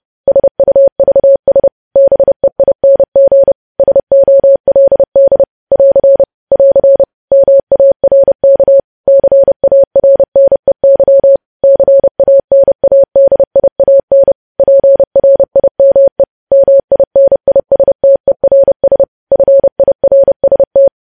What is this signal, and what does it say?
Morse code, heard as a keyed tone, about 30 wpm, 550 Hz. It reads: SUVS BEING SOLD <AR> <AR> MARK CARNEY CANADIAN PRIME MINISTERS FIRST
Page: news headlines in morse code at 30 wpm